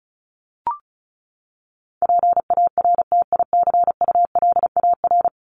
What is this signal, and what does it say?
Morse code, heard as a keyed tone, about 34 wpm, 700 Hz. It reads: PARTICULAR
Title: LY Suffix - Top 100 Words with Base Words 35wpm